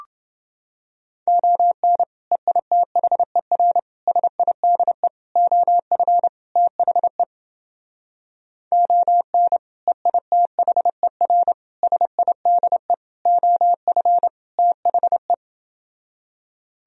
Morse code, heard as a keyed tone, about 31 wpm, 700 Hz. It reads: ON EITHER SIDE OF THE ON EITHER SIDE OF THE